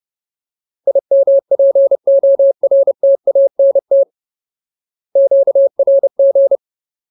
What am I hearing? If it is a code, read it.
Morse code, heard as a keyed tone, about 30 wpm, 550 Hz. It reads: IMPORTANT QRG
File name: Ultimate Word List / 30WPM Ultimate